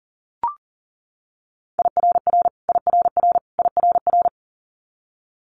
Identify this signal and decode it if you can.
Morse code, heard as a keyed tone, about 38 wpm, 700 Hz. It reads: IRR IRR IRR
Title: Multi-letter Phonemes Repeated 3x 40wpm